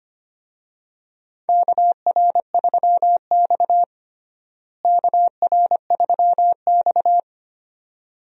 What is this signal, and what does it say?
Morse code, heard as a keyed tone, about 25 wpm, 700 Hz. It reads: KR3X KR3X
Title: USA Call Signs - 4 Characters - Repeated 2x - 25wpm